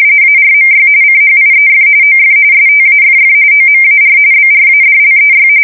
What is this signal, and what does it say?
RTTY, about 45.45 baud. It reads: ARMY CHIEF OF STAFF OMAR BRADLEY
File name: fsk170 / USS-New-Jersey